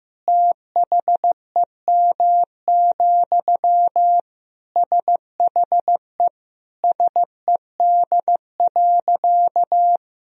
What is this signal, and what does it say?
Morse code, heard as a keyed tone, about 15 wpm, 700 Hz. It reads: THEM, SHE SED.
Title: Sentences from Top 400 Words and Abbreviations 15wpm